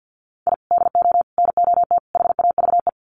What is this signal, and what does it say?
Morse code, heard as a keyed tone, about 50 wpm, 700 Hz. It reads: I DO NOT HAVE
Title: Sets of 4 Words 50wpm